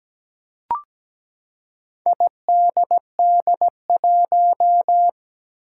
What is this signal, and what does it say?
Morse code, heard as a keyed tone, about 17 wpm, 700 Hz. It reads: IDD1
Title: Four Challenging Letter-Number Combinations 17wpm